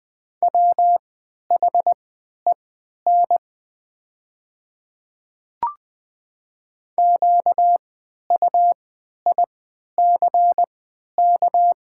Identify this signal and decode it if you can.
Morse code, heard as a keyed tone, about 20 wpm, spaced as for 11 wpm, 700 Hz. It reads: WHEN QUICK
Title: 3x Character Spacing - Top 500 Words - 20wpm